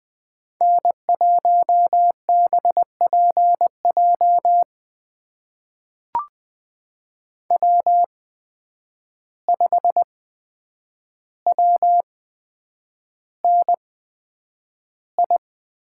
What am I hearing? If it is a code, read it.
Morse code, heard as a keyed tone, about 20 wpm, spaced as for 9 wpm, 700 Hz. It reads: N1BPJ W5WNI
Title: USA Call Signs - 5 Characters - 8x Character Spacing - 20wpm